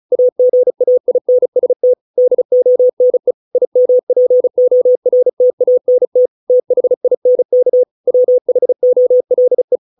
Morse code, heard as a keyed tone, about 34 wpm, 500 Hz. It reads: AGAINST DONE IMPORTANT THINK WHOLE